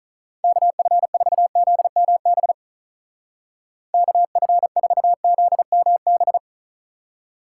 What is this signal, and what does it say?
Morse code, heard as a keyed tone, about 44 wpm, 700 Hz. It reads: KF4ZMB KF4ZMB